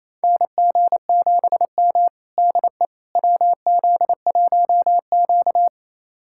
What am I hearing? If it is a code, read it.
Morse code, heard as a keyed tone, about 28 wpm, 700 Hz. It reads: NG7M DE WZ1Q
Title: USA Callsign DE Callsign - 4 Characters - 28wpm